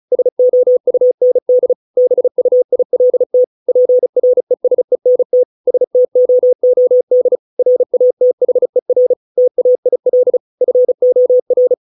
Morse code, heard as a keyed tone, about 34 wpm, 500 Hz. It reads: SOUND BUILT PRESENT STOOD RATHER TAIL FOR
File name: 632 words / 632 Words 35WPM